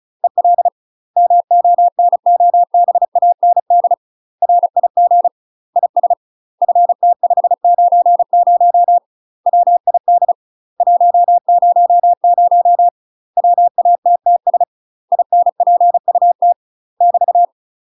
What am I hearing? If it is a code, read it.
Morse code, heard as a keyed tone, about 34 wpm, 700 Hz. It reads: EL MONOBAND RIG IS FT590 WID 100 WATTS INPUT <BT>
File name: UK MASTER QSOs TRAINING / 35WPM / UK MASTER QSOs 0039